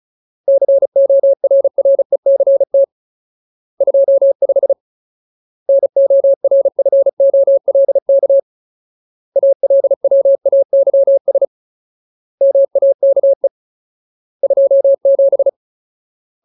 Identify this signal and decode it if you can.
Morse code, heard as a keyed tone, about 34 wpm, 550 Hz. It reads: CORRECT 25 NORFOLK ALWAYS MAKE 27
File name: Ultimate Word List / 35WPM Ultimate